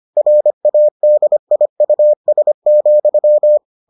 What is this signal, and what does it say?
Morse code, heard as a keyed tone, about 25 wpm, 600 Hz. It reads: RADIUS,